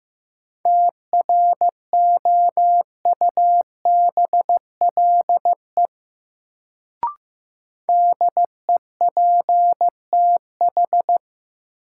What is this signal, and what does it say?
Morse code, heard as a keyed tone, about 15 wpm, 700 Hz. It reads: TROUBLE DEPTH
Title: Sentences from Top 1200 Words - New Word Drill Alt - 15wpm